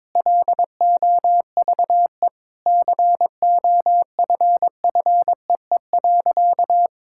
Morse code, heard as a keyed tone, about 21 wpm, 700 Hz. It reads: LOVE COFFEE.
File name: Sentences with 7-Word Phrases - 22wpm